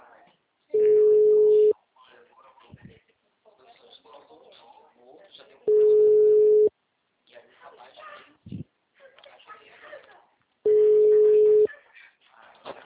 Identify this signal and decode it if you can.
Ringback tone (Europe, 425 Hz), with speech.